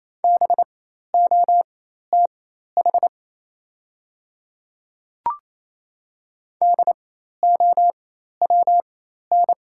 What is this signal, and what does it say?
Morse code, heard as a keyed tone, about 28 wpm, spaced as for 13 wpm, 700 Hz. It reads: BOTH DOWN
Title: 4x Character Spacing - Top 300 Words - 28wpm